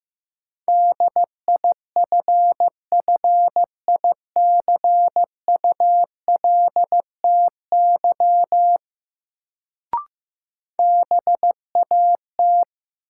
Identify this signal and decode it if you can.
Morse code, heard as a keyed tone, about 15 wpm, 700 Hz. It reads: DIFFICULTY BAT
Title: Sentences from Top 1500 Words - New Word Drill - 15wpm